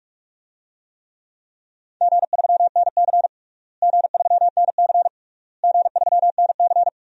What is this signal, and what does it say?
Morse code, heard as a keyed tone, about 46 wpm, 700 Hz. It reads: G3NC G3NC G3NC
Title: DXCC-100 Call Signs - 4 Characters - Repeated 3x - 45wpm